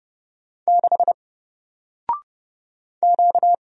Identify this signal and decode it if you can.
Morse code, heard as a keyed tone, about 30 wpm, 700 Hz. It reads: 6 Q
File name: Single-Letter-Number-30wpm